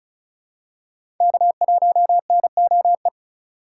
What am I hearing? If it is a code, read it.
Morse code, heard as a keyed tone, about 34 wpm, 700 Hz. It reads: K1NOE
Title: USA Call Signs - 4x Character Spacing - 35wpm